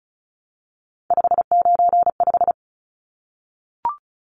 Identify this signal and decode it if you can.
Morse code, heard as a keyed tone, about 34 wpm, 700 Hz. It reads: 595
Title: Numbers - 3-Digit 35wpm